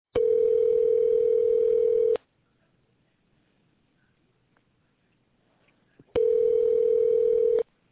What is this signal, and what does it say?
Ringback tone (North America), 440 + 480 Hz.